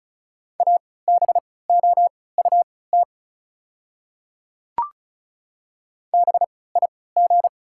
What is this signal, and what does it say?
Morse code, heard as a keyed tone, about 36 wpm, spaced as for 20 wpm, 690 Hz. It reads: ABOUT BIG
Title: 3x Character Spacing - Top 200 Words - 35wpm